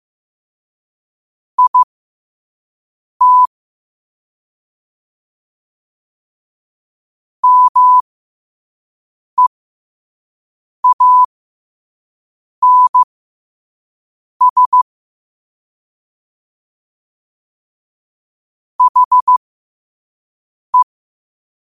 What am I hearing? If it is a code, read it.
Morse code, heard as a keyed tone, about 15 wpm, spaced as for 5 wpm, 1000 Hz. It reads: IT MEANS HE
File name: qotdfiles / 5wpm